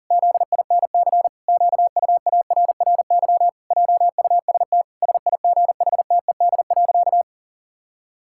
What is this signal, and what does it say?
Morse code, heard as a keyed tone, about 40 wpm, 700 Hz. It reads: ZINC QUARRY JUST SIGHTED.
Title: Pangram Challenge v2 - 40wpm